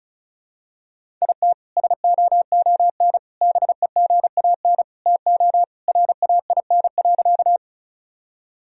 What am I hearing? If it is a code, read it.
Morse code, heard as a keyed tone, about 36 wpm, 700 Hz. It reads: IT SOON BEGAN TO RAIN.
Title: Sentences from Top 1000 Words - Review Part 1 - 35wpm